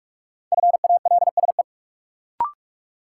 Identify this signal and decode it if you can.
Morse code, heard as a keyed tone, about 46 wpm, 700 Hz. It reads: FALSE
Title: Sentences from Top 1600 Words - New Word Drill - 45wpm